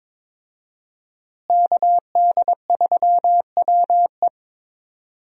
Morse code, heard as a keyed tone, about 22 wpm, 700 Hz. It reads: KD3WE
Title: USA Call Signs - 5 Characters - Repeated 3x - 22wpm